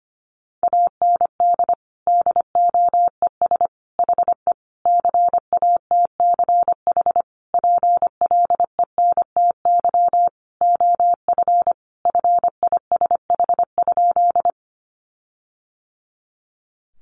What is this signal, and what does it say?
Morse code, heard as a keyed tone, about 25 wpm, 700 Hz. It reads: AND DOES HE CATCH PLENTY OF FISH?
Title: Sentences from Top 1500 Words 25wpm